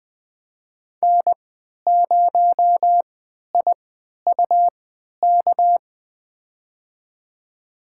Morse code, heard as a keyed tone, about 20 wpm, spaced as for 11 wpm, 700 Hz. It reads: N0IUK
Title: USA Call Signs - 5 Characters - 3x Character Spacing - 20wpm